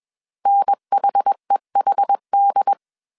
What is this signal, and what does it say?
Morse code, heard as a keyed tone, about 21 wpm, 800 Hz. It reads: NHEHD